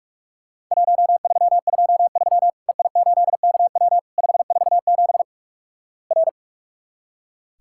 Morse code, heard as a keyed tone, about 46 wpm, 700 Hz. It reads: 1323 EI8KW 547 R
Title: Worked All Europe DX Contest - QTC Exchange - 45wpm